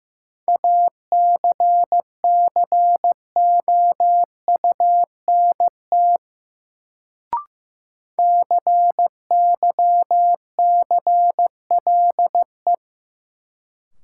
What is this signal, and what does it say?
Morse code, heard as a keyed tone, about 15 wpm, 700 Hz. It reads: ACCOUNT CYCLE
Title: Sentences from Top 900 Words - New Word Drill - 15wpm